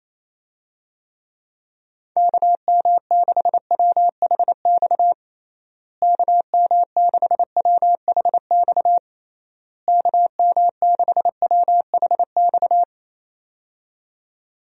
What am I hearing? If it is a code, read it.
Morse code, heard as a keyed tone, about 28 wpm, 700 Hz. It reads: KM6WHX KM6WHX KM6WHX
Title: USA Call Signs - 6 Characters - Repeated 3x - 28wpm